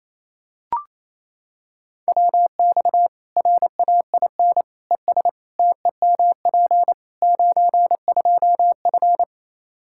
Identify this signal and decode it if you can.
Morse code, heard as a keyed tone, about 28 wpm, 700 Hz. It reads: WX RAIN ES TEMP 92F
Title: Practice QSO - Exchange 2 - Info - 28wpm